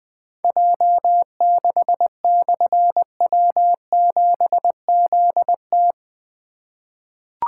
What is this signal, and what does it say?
Morse code, heard as a keyed tone, about 20 wpm, 700 Hz. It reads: J6/W7ZT
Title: Difficult Call Signs 20wpm